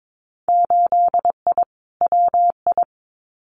Morse code, heard as a keyed tone, about 22 wpm, 700 Hz. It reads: 8I WI
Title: Winter Field Day Exchanges 22wpm